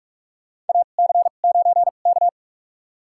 Morse code, heard as a keyed tone, about 44 wpm, spaced as for 32 wpm, 700 Hz. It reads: AC9K
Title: USA Call Signs - 4 Characters - 2x Character Spacing - 45wpm